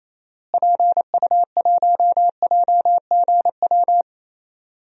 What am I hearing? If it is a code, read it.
Morse code, heard as a keyed tone, about 28 wpm, 700 Hz. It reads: PU1JGW
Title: DXCC-100 Call Signs - 6 Characters - Repeated 2x - 28wpm